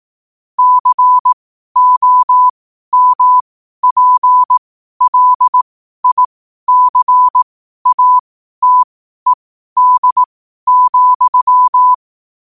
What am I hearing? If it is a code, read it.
Morse code, heard as a keyed tone, about 18 wpm, spaced as for 12 wpm, 1000 Hz. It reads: COMPLICATED,
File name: qotdfiles / 12wpm